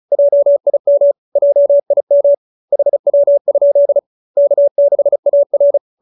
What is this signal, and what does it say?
Morse code, heard as a keyed tone, about 34 wpm, 570 Hz. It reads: JIM JIM HW? K6AR